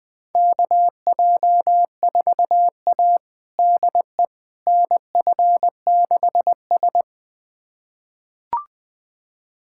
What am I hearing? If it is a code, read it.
Morse code, heard as a keyed tone, about 20 wpm, 700 Hz. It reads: KJ4A DE NF6S
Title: USA Callsign DE Callsign - 4 Characters - 20wpm